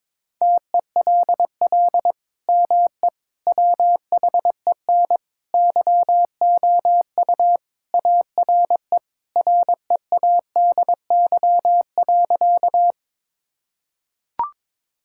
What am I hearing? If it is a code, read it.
Morse code, heard as a keyed tone, about 22 wpm, 700 Hz. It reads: TELL ME WHEN YOU ARE READY.
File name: practice / 22wpm / Sentences with 2-Word Phrases - 22wpm